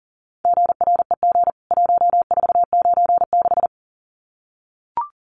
Morse code, heard as a keyed tone, about 40 wpm, 700 Hz. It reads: GREG 1496
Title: CWT Exchange 40wpm